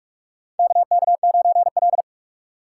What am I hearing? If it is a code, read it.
Morse code, heard as a keyed tone, about 46 wpm, 700 Hz. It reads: KK9L
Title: USA Call Signs - 2x Character Spacing - 45wpm